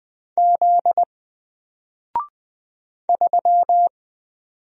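Morse code, heard as a keyed tone, about 20 wpm, 700 Hz. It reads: Z 3
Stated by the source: Single letter and number Morse drill at 20 wpm, from N4LCD.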